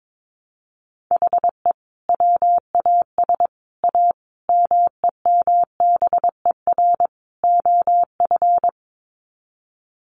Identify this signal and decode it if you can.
Morse code, heard as a keyed tone, about 22 wpm, 700 Hz. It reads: HE WAS A MEMBER OF
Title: Sets of 5 Words Encore2 22wpm